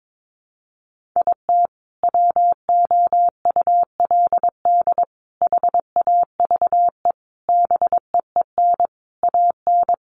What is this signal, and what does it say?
Morse code, heard as a keyed tone, about 22 wpm, 700 Hz. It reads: IT WOULD HAVE BEEN AN
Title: Sentences from Top 1100 Words 22wpm